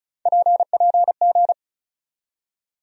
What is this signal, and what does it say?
Morse code, heard as a keyed tone, about 34 wpm, 700 Hz. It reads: PPG
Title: USA Call Sign Postfixes - 3 Letters 35wpm